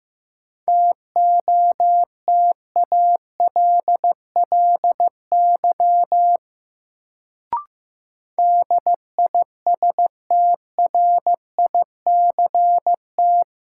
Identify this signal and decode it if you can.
Morse code, heard as a keyed tone, about 15 wpm, 700 Hz. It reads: TOTALLY DISTRICT
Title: Sentences from Top 1600 Words - New Word Drill Alt - 15wpm